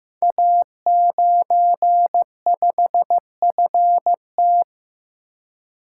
Morse code, heard as a keyed tone, about 15 wpm, 700 Hz. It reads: A95FT
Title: Code Groups - 5 Letter:Numbers - 15wpm